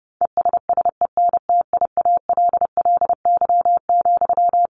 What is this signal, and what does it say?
Morse code, heard as a keyed tone, about 28 wpm, 700 Hz. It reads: ESSENTIALLY,
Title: Sentences from Top 1300 Words 30wpm